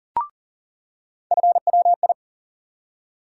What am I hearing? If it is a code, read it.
Morse code, heard as a keyed tone, about 40 wpm, 700 Hz. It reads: FWI
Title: USA Call Sign Postfixes - 3 Letters 40wpm